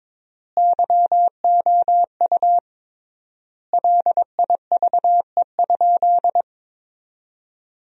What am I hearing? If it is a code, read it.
Morse code, heard as a keyed tone, about 22 wpm, 700 Hz. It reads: YOU LIVE?
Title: 3x Word-Spacing - Sentences from Top 200 Words - 22wpm